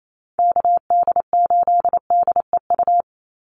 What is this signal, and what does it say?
Morse code, heard as a keyed tone, about 28 wpm, 700 Hz. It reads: KD8DEU
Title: NA SOTA Chaser Callsigns 28wpm